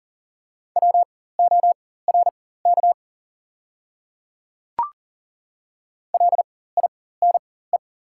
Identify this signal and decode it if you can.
Morse code, heard as a keyed tone, about 40 wpm, spaced as for 19 wpm, 700 Hz. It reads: WORK LINE
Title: 4x Character Spacing - Top 200 Words - 40wpm